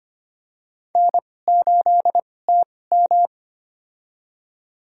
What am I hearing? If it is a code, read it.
Morse code, heard as a keyed tone, about 25 wpm, spaced as for 18 wpm, 700 Hz. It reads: N8TM
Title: USA Call Signs - 4 Characters - 2x Character Spacing - 25wpm